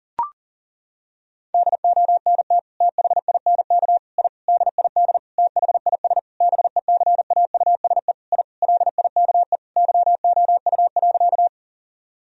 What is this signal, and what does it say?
Morse code, heard as a keyed tone, about 37 wpm, 700 Hz. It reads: DONT THINK I DID THIS BECAUSE I LIKE YOU.
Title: Sentences from Top 1000 Words - Review Part 3 - 40wpm